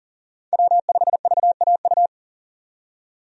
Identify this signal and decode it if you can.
Morse code, heard as a keyed tone, about 40 wpm, 700 Hz. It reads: W5VAU
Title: USA Call Signs - 5 Characters - 40wpm